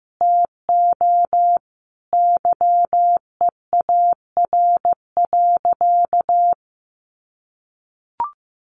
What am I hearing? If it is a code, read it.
Morse code, heard as a keyed tone, about 15 wpm, 700 Hz. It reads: TO YEAR.
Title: Sentences from Top 1200 Words 15wpm